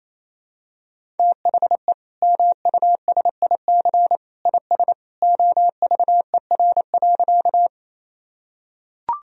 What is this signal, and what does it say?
Morse code, heard as a keyed tone, about 28 wpm, 700 Hz. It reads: THE MUSIC IS OVER.
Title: Sentences from Top 1000 Words - Review Part 1 - 28wpm